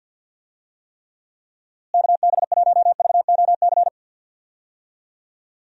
Morse code, heard as a keyed tone, about 47 wpm, 700 Hz. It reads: KB1VOC
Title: USA Call Signs - 6 Characters - Repeated 2x - 50wpm